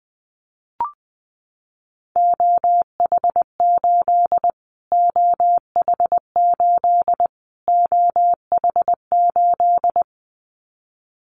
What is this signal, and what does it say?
Morse code, heard as a keyed tone, about 20 wpm, 700 Hz. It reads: OH8 OH8 OH8
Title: Callsign Prefixes for Top 25 DXCC - Repeated 3x - 20wpm